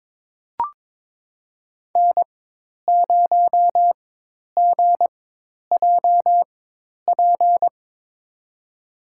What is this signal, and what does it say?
Morse code, heard as a keyed tone, about 22 wpm, spaced as for 10 wpm, 700 Hz. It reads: N0GJP
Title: USA Call Signs - 5 Characters - 4x Character Spacing - 22wpm